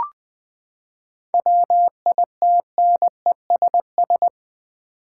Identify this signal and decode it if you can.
Morse code, heard as a keyed tone, about 20 wpm, 700 Hz. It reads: WITNESS
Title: Seven Letter Words - 20wpm